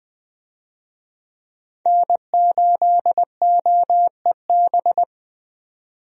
Morse code, heard as a keyed tone, about 20 wpm, 700 Hz. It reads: N8OEB
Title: USA Call Signs - 5 Characters - 20wpm